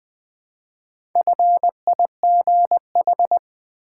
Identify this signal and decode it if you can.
Morse code, heard as a keyed tone, about 20 wpm, 700 Hz. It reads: FIGH
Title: Build A Word - 1 Letter At A Time - 5-Letter Words - 20wpm